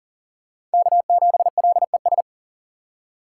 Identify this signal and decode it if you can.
Morse code, heard as a keyed tone, about 40 wpm, 700 Hz. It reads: K7LES
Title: USA Call Signs - 5 Characters - 3x Character Spacing - 40wpm